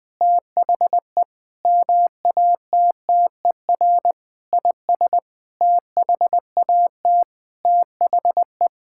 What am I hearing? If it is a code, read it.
Morse code, heard as a keyed tone, about 20 wpm, 700 Hz. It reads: THE MATTER IS THAT THE
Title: Sentences with 9-Word Phrases - 20wpm